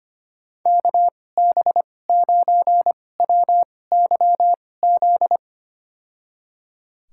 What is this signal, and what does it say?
Morse code, heard as a keyed tone, about 25 wpm, spaced as for 18 wpm, 700 Hz. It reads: KB9WYZ
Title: USA Call Signs - 6 Characters - 2x Character Spacing - 25wpm